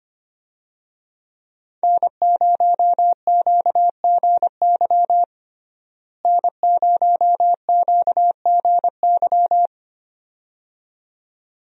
Morse code, heard as a keyed tone, about 25 wpm, 700 Hz. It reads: N0QGY N0QGY
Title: USA Call Signs - 5 Characters - Repeated 2x - 25wpm